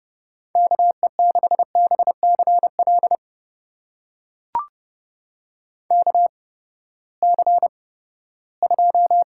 Morse code, heard as a keyed tone, about 30 wpm, spaced as for 17 wpm, 700 Hz. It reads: KE6BCL KC2
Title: USA Call Signs - 6 Characters - 8x Character Spacing - 30wpm